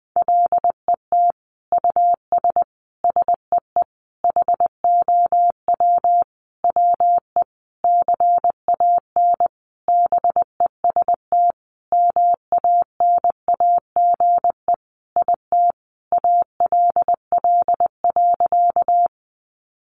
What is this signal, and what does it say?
Morse code, heard as a keyed tone, about 20 wpm, 700 Hz. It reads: LET US SEE HOW WE CAN BEST MANAGE IT ALL.
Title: Sentences from Top 1400 Words 20wpm